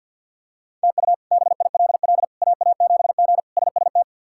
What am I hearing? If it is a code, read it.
Morse code, heard as a keyed tone, about 50 wpm, 700 Hz. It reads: TU BILL AA7G SST
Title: K1USN SST QSO Practice - 50wpm